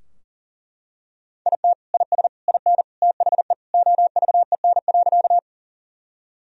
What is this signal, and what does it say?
Morse code, heard as a keyed tone, about 40 wpm, 700 Hz. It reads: IT IS IN THE OVEN.
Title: Sentences from Top 1000 Words - Review Part 4 - 40wpm